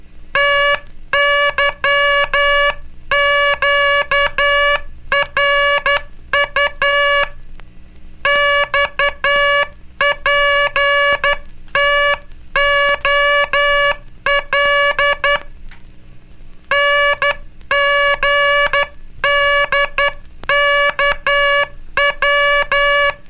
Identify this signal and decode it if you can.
Morse code, heard as a keyed tone, about 12 wpm, 1700 Hz, with noise. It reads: TYQRU XPTOL NGDKW
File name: cursos / Curso CW mp3 / CW Aula - 17